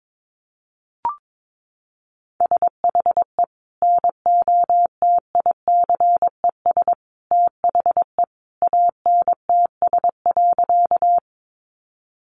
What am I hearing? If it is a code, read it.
Morse code, heard as a keyed tone, about 22 wpm, 700 Hz. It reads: SHE NOTICES THE ANTS.
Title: Easy Phrases 22wpm